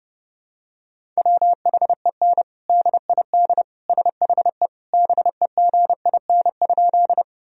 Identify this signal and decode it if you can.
Morse code, heard as a keyed tone, about 32 wpm, 700 Hz. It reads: WHEN DID SHE BEGIN?
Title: Sentences from Top 1000 Words - Review Part 1 - 30wpm